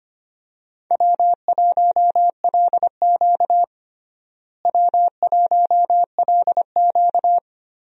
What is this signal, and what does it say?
Morse code, heard as a keyed tone, about 25 wpm, 700 Hz. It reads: W1LQ W1LQ
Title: USA Call Signs - 4 Characters - Repeated 3x - 25wpm